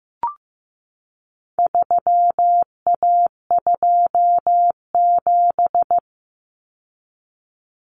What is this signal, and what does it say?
Morse code, heard as a keyed tone, about 15 wpm, 700 Hz. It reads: 3A27
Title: Common Cut Numbers - 4-Digit 15wpm